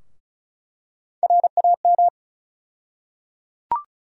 Morse code, heard as a keyed tone, about 34 wpm, 700 Hz. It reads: RAM
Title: USA Call Sign Postfixes - 3 Letters 35wpm